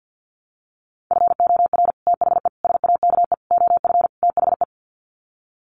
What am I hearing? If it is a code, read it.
Morse code, heard as a keyed tone, about 50 wpm, 700 Hz. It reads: FOR THE SAKE OF THE
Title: Sets of 5 Words Encore 50wpm